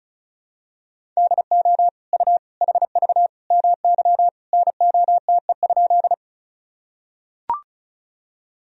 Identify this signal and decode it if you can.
Morse code, heard as a keyed tone, about 34 wpm, 700 Hz. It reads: DO U HV MY NOTE?
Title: Sentences from Top 500 Words and Abbreviations 35wpm